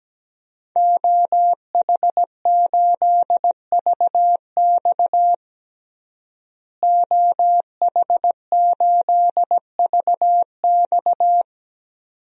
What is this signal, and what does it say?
Morse code, heard as a keyed tone, about 17 wpm, 700 Hz. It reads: OH8VX OH8VX
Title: DXCC-100 Call Signs - 5 Characters - Repeated 2x - 17wpm